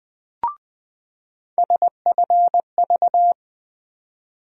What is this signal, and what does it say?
Morse code, heard as a keyed tone, about 20 wpm, 700 Hz. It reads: SFV
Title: USA Call Sign Postfixes - 3 Letters 20wpm